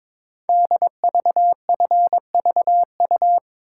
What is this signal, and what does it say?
Morse code, heard as a keyed tone, about 22 wpm, 690 Hz. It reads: DVFVU
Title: Code Groups - 5 Letters - 22wpm